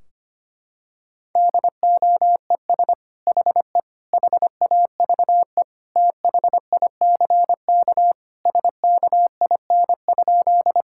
Morse code, heard as a keyed tone, about 25 wpm, 700 Hz. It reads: DOES HE HAVE THICK SKIN?
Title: Sentences from Top 1600 Words 25wpm